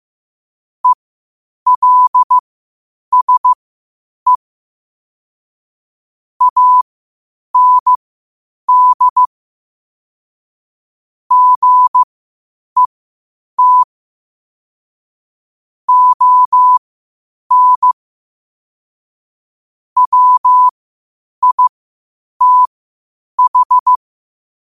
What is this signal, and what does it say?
Morse code, heard as a keyed tone, about 15 wpm, spaced as for 8 wpm, 1000 Hz. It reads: ELSE AND GET ON WITH
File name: qotdfiles / 8wpm